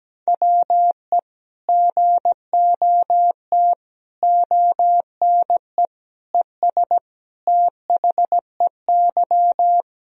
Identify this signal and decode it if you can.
Morse code, heard as a keyed tone, about 17 wpm, 700 Hz. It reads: WE GOT ONE ES THEY
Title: Sentences from Top 300 Words and Abbreviations 17wpm